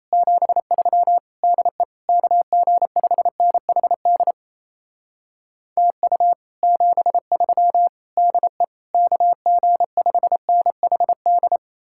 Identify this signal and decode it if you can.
Morse code, heard as a keyed tone, about 37 wpm, 700 Hz. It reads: 73 DE KG5NHD TU 73 DE KG5NHD